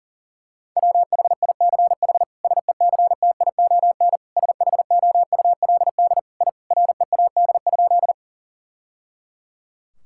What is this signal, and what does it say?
Morse code, heard as a keyed tone, about 37 wpm, 700 Hz. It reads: WHICH SECTION SHOULD I READ?
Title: Sentences from Top 1000 Words - Review Part 5 - 40wpm